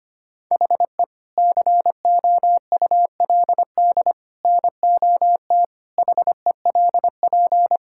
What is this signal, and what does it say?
Morse code, heard as a keyed tone, about 25 wpm, 700 Hz. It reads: HE COULD NOT HELP
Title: Sentences from Top 1600 Words 25wpm